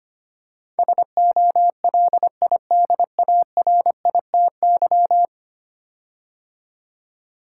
Morse code, heard as a keyed tone, about 25 wpm, 700 Hz. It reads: SOLIDARITY
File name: 10 Letter Words - First 3 Letters Given - 25wpm